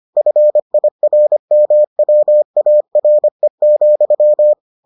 Morse code, heard as a keyed tone, about 25 wpm, 600 Hz. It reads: FIRMWARE,